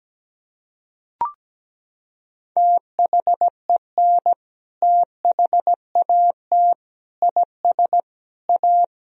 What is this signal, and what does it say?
Morse code, heard as a keyed tone, about 17 wpm, 700 Hz. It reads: THEN THAT IS A
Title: Sentences from Top 1000 Words - Review Part 4 - 17wpm